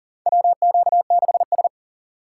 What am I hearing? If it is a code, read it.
Morse code, heard as a keyed tone, about 40 wpm, 700 Hz. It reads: WQ6S
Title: USA Call Signs - 4 Characters - 3x Character Spacing - 40wpm